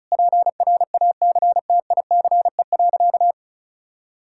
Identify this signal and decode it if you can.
Morse code, heard as a keyed tone, about 34 wpm, 700 Hz. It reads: PRACTICE.
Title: Sentences from Top 1000 Words - Review Part 5 - 35wpm